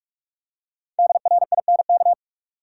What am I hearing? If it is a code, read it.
Morse code, heard as a keyed tone, about 45 wpm, 700 Hz. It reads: DRINK
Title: Five Letter Words - 45wpm